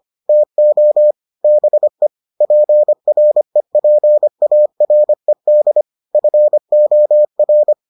Morse code, heard as a keyed tone, about 25 wpm, 600 Hz. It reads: TO BE PREPARED FOR